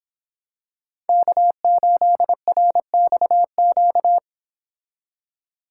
Morse code, heard as a keyed tone, about 25 wpm, 700 Hz. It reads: K8RXQ